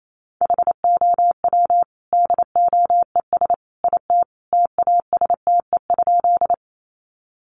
Morse code, heard as a keyed tone, about 28 wpm, 700 Hz. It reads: HOW DOES IT TASTE?
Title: Sentences from Top 1100 Words 28wpm